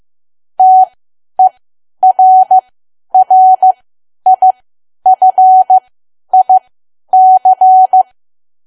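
Morse code, heard as a keyed tone, about 15 wpm, spaced as for 10 wpm, 750 Hz. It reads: TERRIFIC
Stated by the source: Morse learning practice at 10 wpm.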